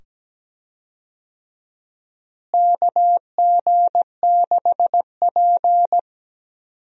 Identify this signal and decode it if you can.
Morse code, heard as a keyed tone, about 17 wpm, 700 Hz. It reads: KG6P
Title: USA Call Signs - 4 Characters - Repeated 2x - 17wpm